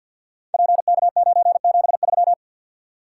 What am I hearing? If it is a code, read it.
Morse code, heard as a keyed tone, about 52 wpm, 700 Hz. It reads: PK973
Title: Code Groups - 5 Letter:Numbers - 50wpm